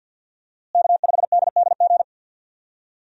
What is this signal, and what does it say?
Morse code, heard as a keyed tone, about 51 wpm, 700 Hz. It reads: K5DDG
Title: USA Call Signs - 5 Characters - Repeated 2x - 50wpm